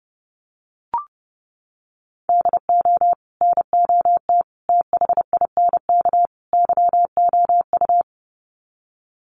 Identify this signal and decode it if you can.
Morse code, heard as a keyed tone, about 30 wpm, 700 Hz. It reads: DO NOT THINK YOU
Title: Sets of 4 Words 30wpm